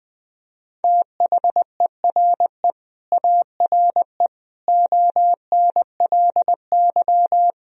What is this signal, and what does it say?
Morse code, heard as a keyed tone, about 21 wpm, 700 Hz. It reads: THERE ARE ONLY
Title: Sentences from Top 1000 Words - Review Part 1 - 20wpm